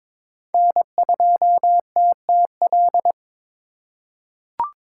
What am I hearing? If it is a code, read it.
Morse code, heard as a keyed tone, about 22 wpm, 700 Hz. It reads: N2TTL
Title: USA Call Signs - 5 Characters - 4x Character Spacing - 22wpm